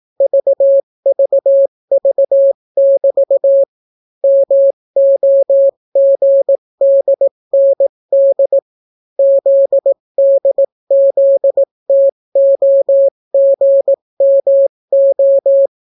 Morse code, heard as a keyed tone, about 18 wpm, 550 Hz. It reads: VVV<BT> MOGDND ZDZTOGMO